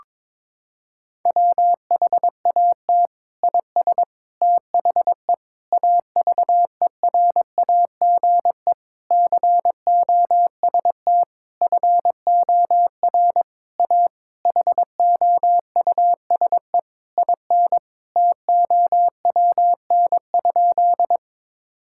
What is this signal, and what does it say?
Morse code, heard as a keyed tone, about 22 wpm, 700 Hz. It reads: WHAT IS THE AVERAGE COST FOR A HOUSE IN TOWN?
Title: Sentences from Top 1000 Words - Review Part 6 - 22wpm